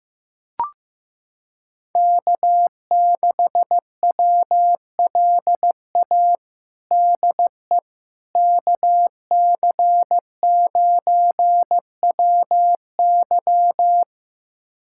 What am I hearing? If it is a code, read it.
Morse code, heard as a keyed tone, about 15 wpm, 700 Hz. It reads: K6WLA DE KC9WY
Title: USA Callsign DE Callsign 15wpm